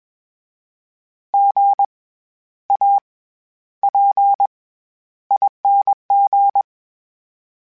Morse code, heard as a keyed tone, about 21 wpm, 800 Hz. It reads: G A P ING
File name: Special INGs-mp3s / Special INGs-mp3s / SlowToFaster ing @20